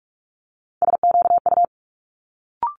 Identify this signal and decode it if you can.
Morse code, heard as a keyed tone, about 46 wpm, 700 Hz. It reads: SQU
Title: Multi-letter Phonemes Quick Fire 45wpm